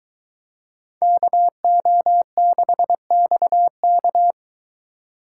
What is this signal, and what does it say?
Morse code, heard as a keyed tone, about 22 wpm, 700 Hz. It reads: KO6XK